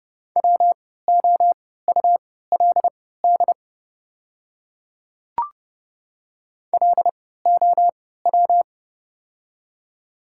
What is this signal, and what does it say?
Morse code, heard as a keyed tone, about 30 wpm, spaced as for 17 wpm, 690 Hz. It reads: WOULD LOW
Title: 3x Character Spacing - Top 500 Words - 30wpm